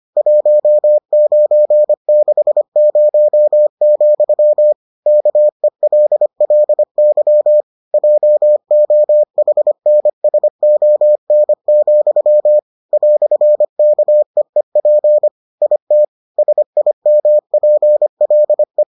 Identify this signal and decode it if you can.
Morse code, heard as a keyed tone, about 25 wpm, 600 Hz. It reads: 1960, KELLY JOHNSON, "KEEP IT SIMPLE